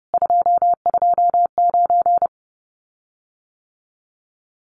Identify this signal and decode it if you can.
Morse code, heard as a keyed tone, about 30 wpm, 700 Hz. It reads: 229
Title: Common QSO Phrases Encore 30wpm